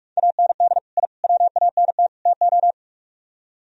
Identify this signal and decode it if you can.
Morse code, heard as a keyed tone, about 57 wpm, 700 Hz. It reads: AND I WANT TO